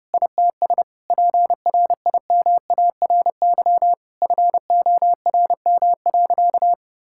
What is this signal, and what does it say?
Morse code, heard as a keyed tone, about 32 wpm, 700 Hz. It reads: ITS PRIMARY FORM.
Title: Sentences from Top 1000 Words - Review Part 5 - 30wpm